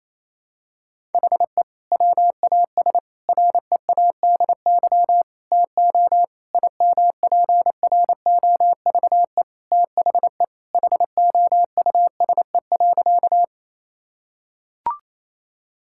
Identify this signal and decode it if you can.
Morse code, heard as a keyed tone, about 28 wpm, 700 Hz. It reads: HE WAS READY TO IMPROVE THE HOUSE.
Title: Sentences from Top 1000 Words - Review Part 5 - 28wpm